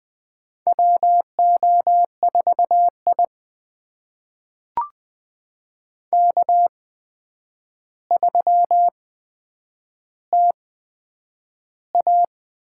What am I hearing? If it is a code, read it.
Morse code, heard as a keyed tone, about 20 wpm, spaced as for 9 wpm, 700 Hz. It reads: WO4I K3TA
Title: USA Call Signs - 4 Characters - 8x Character Spacing - 20wpm